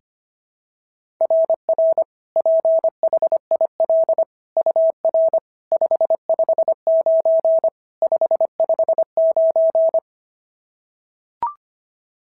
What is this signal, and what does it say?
Morse code, heard as a keyed tone, about 25 wpm, 690 Hz. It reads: RR PHIL UR 559 559
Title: SOTA QSO Practice 25wpm